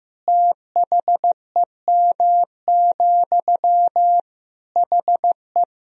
Morse code, heard as a keyed tone, about 15 wpm, 700 Hz. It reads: THEM, HE
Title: Sentences from Top 400 Words and Abbreviations 15wpm